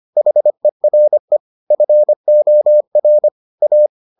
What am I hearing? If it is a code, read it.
Morse code, heard as a keyed tone, about 25 wpm, 600 Hz. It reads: HERE FOR A